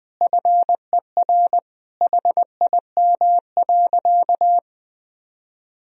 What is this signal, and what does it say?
Morse code, heard as a keyed tone, about 20 wpm, 700 Hz. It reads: FER HIM.
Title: Sentences from Top 500 Words and Abbreviations 20wpm